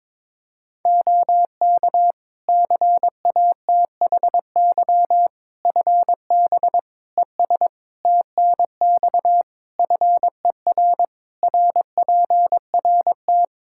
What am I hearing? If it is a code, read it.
Morse code, heard as a keyed tone, about 22 wpm, 700 Hz. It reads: OK CATHY FB ES TNX FER RPRT
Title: Practice QSO - Exchange 2 - Info - 22wpm